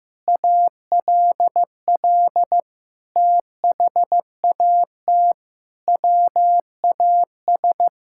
Morse code, heard as a keyed tone, about 15 wpm, 700 Hz. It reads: ALL THAT WAS
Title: Sentences from Top 1600 Words 15wpm